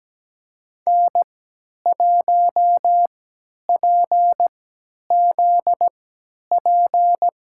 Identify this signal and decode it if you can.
Morse code, heard as a keyed tone, about 17 wpm, spaced as for 10 wpm, 700 Hz. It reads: N1PZP
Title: USA Call Signs - 5 Characters - 3x Character Spacing - 17wpm